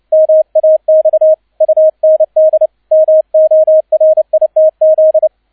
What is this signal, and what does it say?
Morse code, heard as a keyed tone, about 29 wpm, 630 Hz, with radio noise. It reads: MAX UND MORITZ
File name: 002 Zweiter Streich 30wpm